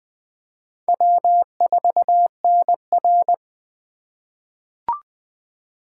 Morse code, heard as a keyed tone, about 20 wpm, 700 Hz. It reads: W4NR
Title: USA Call Signs - 4x Character Spacing - 20wpm